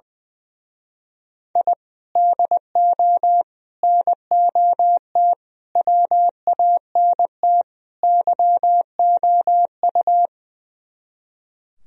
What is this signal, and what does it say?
Morse code, heard as a keyed tone, about 20 wpm, 700 Hz. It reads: I DO NOT WANT YOU
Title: Speed-Racing - Sets of 5 Words 20wpm